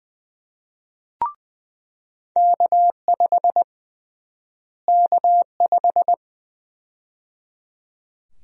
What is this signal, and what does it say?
Morse code, heard as a keyed tone, about 20 wpm, 700 Hz. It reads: K5 K5
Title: USA Call Sign Prefixes - Repeated 2x - 20wpm